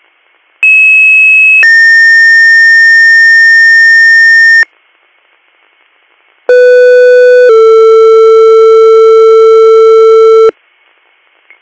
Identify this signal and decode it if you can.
Two-tone page, heard as a one-second tone, then a three-second tone, about 2600 then 1800 Hz, about 500 then 450 Hz. It is heard from 0.6 to 10.5 s.